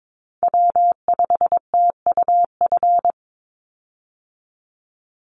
Morse code, heard as a keyed tone, about 22 wpm, 700 Hz. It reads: W5TUF
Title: Call Signs Repeated 3x - 22wpm